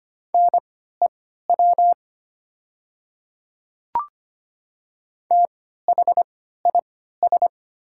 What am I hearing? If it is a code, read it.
Morse code, heard as a keyed tone, about 25 wpm, spaced as for 14 wpm, 690 Hz. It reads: NEW THIS
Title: 3x Character Spacing - Top 200 Words - 25wpm